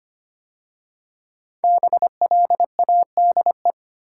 Morse code, heard as a keyed tone, about 25 wpm, 700 Hz. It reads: BLADE
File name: 5 Letter Words - First 3 Letters Given - 25wpm